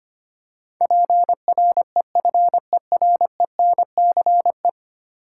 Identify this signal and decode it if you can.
Morse code, heard as a keyed tone, about 25 wpm, 700 Hz. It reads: PREFERENCE
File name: 10 Letter Words - First 3 Letters Given - 25wpm